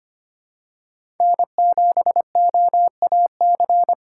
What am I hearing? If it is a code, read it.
Morse code, heard as a keyed tone, about 25 wpm, 700 Hz. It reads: N7OAC
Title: USA Call Signs - 5 Characters - 4x Character Spacing - 25wpm